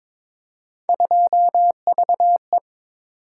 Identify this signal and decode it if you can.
Morse code, heard as a keyed tone, about 22 wpm, 700 Hz. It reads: 2VE
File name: practice / 22wpm / Three Letter-Number Combinations - 22wpm